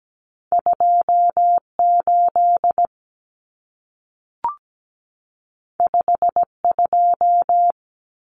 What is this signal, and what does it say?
Morse code, heard as a keyed tone, about 17 wpm, 700 Hz. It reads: 28 52
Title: Numbers - 2-Digit 17wpm